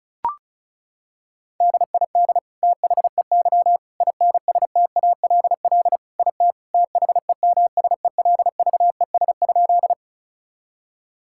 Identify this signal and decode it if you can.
Morse code, heard as a keyed tone, about 36 wpm, 700 Hz. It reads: DID THEY INSTALL IT THEMSELVES?
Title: Sentences from Top 1600 Words 35wpm